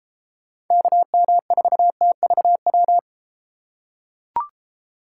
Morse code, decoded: KM4TVW

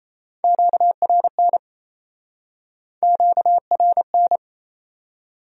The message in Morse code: QRN QRN

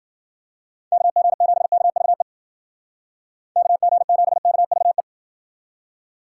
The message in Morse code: KG7KVE KG7KVE

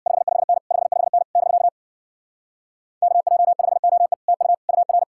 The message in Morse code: 55N 55N <BK> KJ5OE TU UR

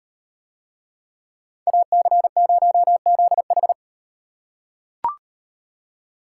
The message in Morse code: AC0ZH

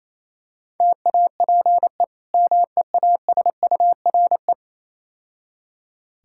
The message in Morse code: TAPE MEASURE

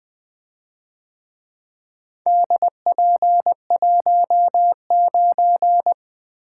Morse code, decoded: DP19

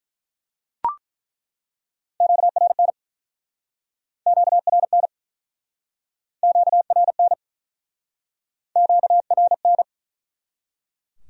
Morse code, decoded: QRN QRN QRN QRN

